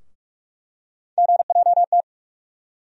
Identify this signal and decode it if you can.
Morse code, heard as a keyed tone, about 46 wpm, 700 Hz. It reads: GJT
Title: USA Call Sign Postfixes - 3 Letters 45wpm